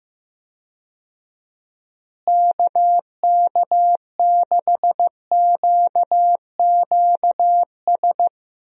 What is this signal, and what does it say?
Morse code, heard as a keyed tone, about 16 wpm, 700 Hz. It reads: KK6QQS
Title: USA Call Signs - 6 Characters - Repeated 2x - 15wpm